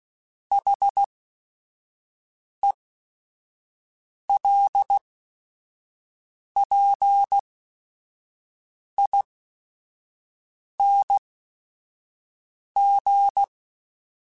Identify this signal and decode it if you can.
Morse code, heard as a keyed tone, about 16 wpm, spaced as for 5 wpm, 800 Hz. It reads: HELPING